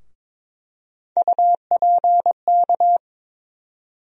Morse code, heard as a keyed tone, about 22 wpm, 700 Hz. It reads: UPK
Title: USA Call Sign Postfixes - 3 Letters 22wpm